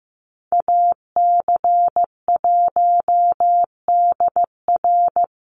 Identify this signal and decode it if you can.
Morse code, heard as a keyed tone, about 15 wpm, 700 Hz. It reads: AC1DR
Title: USA Call Signs 15wpm